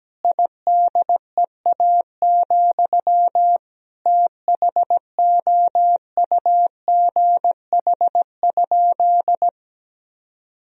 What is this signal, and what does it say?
Morse code, heard as a keyed tone, about 17 wpm, 700 Hz. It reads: IDEA, THOUGH?
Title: Sentences from Top 1000 Words - Review Part 2 - 17wpm